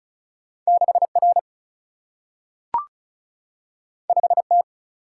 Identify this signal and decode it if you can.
Morse code, heard as a keyed tone, about 36 wpm, 700 Hz. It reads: 6 R 5 T
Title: Single-Letter Single-Number 35wpm